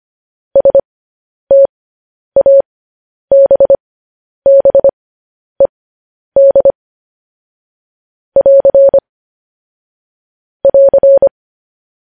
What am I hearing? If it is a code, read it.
Morse code, heard as a keyed tone, about 25 wpm, spaced as for 10 wpm, 550 Hz. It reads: STABBED <AR> <AR>
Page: News Headlines in Morse Code at 10 WPM